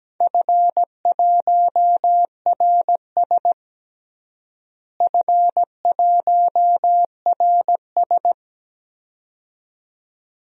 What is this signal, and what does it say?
Morse code, heard as a keyed tone, about 17 wpm, 700 Hz. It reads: F1RS F1RS
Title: DXCC-100 Call Signs - 4 Characters - Repeated 2x - 17wpm